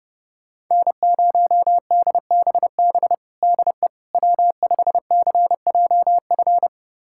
Morse code, heard as a keyed tone, about 30 wpm, 700 Hz. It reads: N0DBB DE W5CJF
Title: USA Callsign DE Callsign - 5 Characters - 30wpm